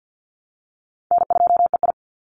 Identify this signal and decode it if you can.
Morse code, heard as a keyed tone, about 50 wpm, 700 Hz. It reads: N2EI
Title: Easy Call Signs - 50wpm